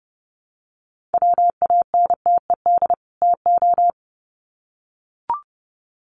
Morse code, heard as a keyed tone, about 30 wpm, 700 Hz. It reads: WANTED TO
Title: Sets of 2 Words 30wpm